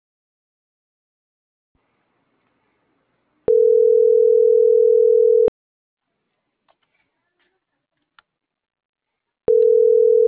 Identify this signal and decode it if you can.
Ringback tone (North America), 440 + 480 Hz.